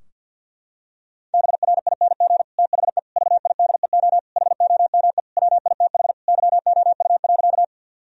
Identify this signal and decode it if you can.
Morse code, heard as a keyed tone, about 50 wpm, 700 Hz. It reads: BRING THE VIDEO HOME WITH YOU.
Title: Sentences from Top 1000 Words - Review Part 4 - 50wpm